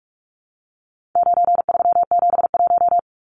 Morse code, heard as a keyed tone, about 44 wpm, 700 Hz. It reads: 9371
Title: Numbers - 4-Digit 45wpm